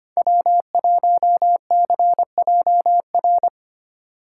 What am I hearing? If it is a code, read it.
Morse code, heard as a keyed tone, about 25 wpm, 700 Hz. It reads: W1CJR
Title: USA Call Signs - 5 Characters - 3x Character Spacing - 25wpm